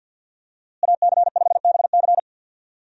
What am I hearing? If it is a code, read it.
Morse code, heard as a keyed tone, about 48 wpm, 700 Hz. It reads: AC5BC
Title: USA Call Signs - 5 Characters - Repeated 2x - 50wpm